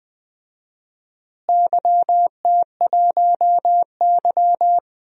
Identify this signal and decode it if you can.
Morse code, heard as a keyed tone, about 20 wpm, 700 Hz. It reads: YT1Y
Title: DXCC-100 Call Signs - 4 Characters - Repeated 2x - 20wpm